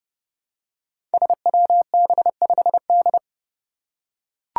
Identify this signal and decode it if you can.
Morse code, heard as a keyed tone, about 28 wpm, 700 Hz. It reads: SWB5D
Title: Five Challenging Letter-Number Combinations 30wpm